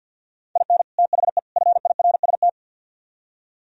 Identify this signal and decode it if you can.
Morse code, heard as a keyed tone, about 60 wpm, 700 Hz. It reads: IN THE FIRST
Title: Speed-Racing - Sets of 3 Words 50wpm